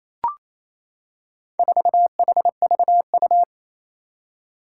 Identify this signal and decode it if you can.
Morse code, heard as a keyed tone, about 27 wpm, 700 Hz. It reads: 4HVU
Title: Four Challenging Letter-Number Combinations 28wpm